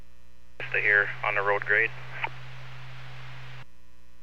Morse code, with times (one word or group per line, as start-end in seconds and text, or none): none